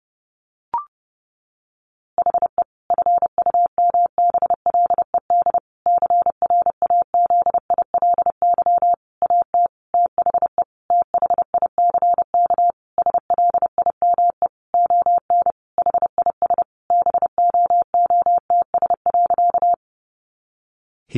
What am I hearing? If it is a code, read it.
Morse code, heard as a keyed tone, about 30 wpm, 700 Hz. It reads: HE FUMBLED CRAZILY AT THE THICK SLIME ON HIS BOOTS.
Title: A Sound Of Thunder 30wpm